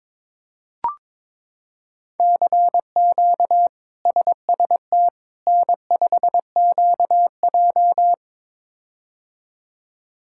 Morse code, heard as a keyed tone, about 22 wpm, 700 Hz. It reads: CQ SST N5QJ
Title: K1USN SST QSO Practice - 22wpm